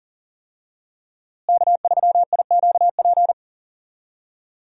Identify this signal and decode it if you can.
Morse code, heard as a keyed tone, about 40 wpm, 700 Hz. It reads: K3IQP
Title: USA Call Signs - 5 Characters - Repeated 3x - 40wpm